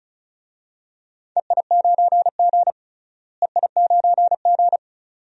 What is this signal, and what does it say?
Morse code, heard as a keyed tone, about 36 wpm, 700 Hz. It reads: EI9G EI9G
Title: DXCC-100 Call Signs - 4 Characters - Repeated 2x - 35wpm